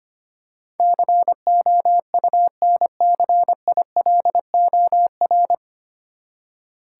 Morse code, heard as a keyed tone, about 25 wpm, 700 Hz. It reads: COUNCILOR
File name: Nine Letter Words - 25wpm